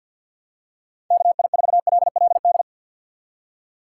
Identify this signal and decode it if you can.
Morse code, heard as a keyed tone, about 48 wpm, 700 Hz. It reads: KI4LLD
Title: USA Call Signs - 6 Characters - 6x Character Spacing - 50wpm